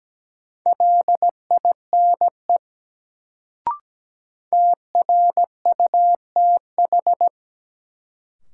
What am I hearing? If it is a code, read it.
Morse code, heard as a keyed tone, about 17 wpm, 700 Hz. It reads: LINE TRUTH
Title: Sets of 5 Words - New Word Drill - 17wpm